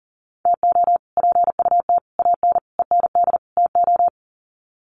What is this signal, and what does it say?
Morse code, heard as a keyed tone, about 40 wpm, 700 Hz. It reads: TO PUT AN END TO
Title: Sets of 5 Words Encore2 40wpm